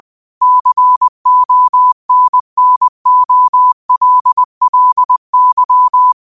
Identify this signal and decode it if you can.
Morse code, heard as a keyed tone, about 20 wpm, 1000 Hz. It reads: CONNOLLY
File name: qotdfiles / 20wpm